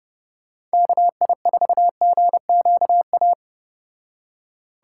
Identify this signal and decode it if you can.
Morse code, heard as a keyed tone, about 32 wpm, 700 Hz. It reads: KI4GQA